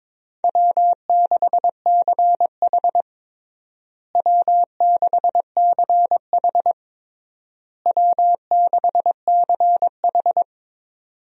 Morse code, heard as a keyed tone, about 22 wpm, 700 Hz. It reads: W6CH W6CH W6CH
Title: USA Call Signs - 4 Characters - Repeated 3x - 22wpm